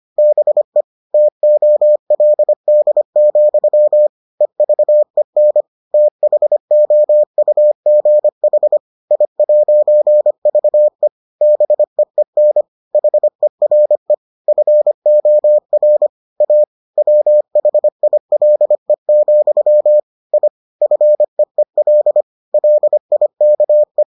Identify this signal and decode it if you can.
Morse code, heard as a keyed tone, about 25 wpm, 600 Hz. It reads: BE TOLD, EVEN THOUGH I'VE BEEN HERE FOR A WHILE, I FEEL LIKE